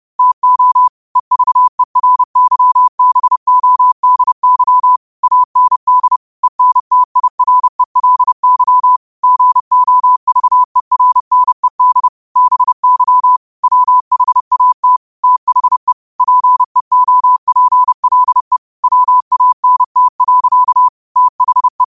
Morse code, heard as a keyed tone, about 30 wpm, 1000 Hz. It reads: TO EVERYBODY AND ENTIRELY GOVERNED BY WHAT THE PEOPLE WANT. THE